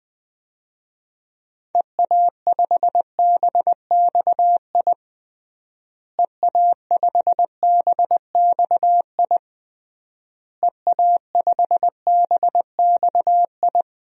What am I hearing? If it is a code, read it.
Morse code, heard as a keyed tone, about 20 wpm, 700 Hz. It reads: EA5BXI EA5BXI EA5BXI
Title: DXCC-100 Call Signs - 6 Characters - Repeated 3x - 20wpm